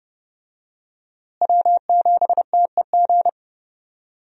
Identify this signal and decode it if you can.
Morse code, heard as a keyed tone, about 32 wpm, 700 Hz. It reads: W7TEG
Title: USA Call Signs - 5 Characters - Repeated 3x - 30wpm